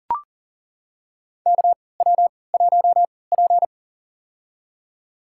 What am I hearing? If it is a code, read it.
Morse code, heard as a keyed tone, about 40 wpm, spaced as for 23 wpm, 700 Hz. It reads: KW1P
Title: USA Call Signs - 4 Characters - 3x Character Spacing - 40wpm